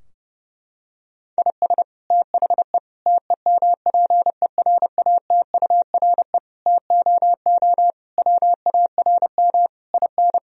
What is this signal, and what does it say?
Morse code, heard as a keyed tone, about 30 wpm, 700 Hz. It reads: IS THE TEMPERATURE TOO WARM IN
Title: Sentences from Top 1000 Words - Review Part 4 - 30wpm